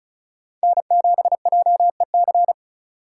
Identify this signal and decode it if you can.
Morse code, heard as a keyed tone, about 34 wpm, 700 Hz. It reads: N7JEC
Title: USA Call Signs - 5 Characters - 4x Character Spacing - 35wpm